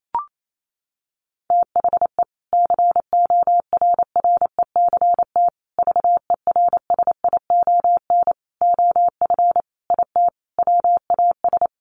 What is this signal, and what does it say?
Morse code, heard as a keyed tone, about 28 wpm, 700 Hz. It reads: THE CORRECT VERSION OF IT WAS
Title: Sentences from Top 1400 Words 28wpm